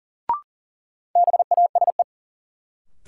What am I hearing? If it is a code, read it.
Morse code, heard as a keyed tone, about 40 wpm, 700 Hz. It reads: BASE
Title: Sets of 5 Words - New Word Drill - 40wpm